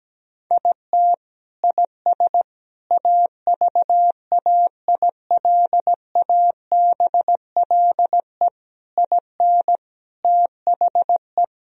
Morse code, heard as a keyed tone, about 17 wpm, 700 Hz. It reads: IT IS AVAILABLE IN THE
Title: Sentences from Top 1000 Words - Review Part 3 - 17wpm